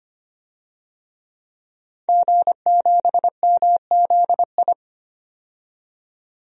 Morse code, heard as a keyed tone, about 25 wpm, 700 Hz. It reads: G7MZI
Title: DXCC-100 Call Signs - 5 Characters - 25wpm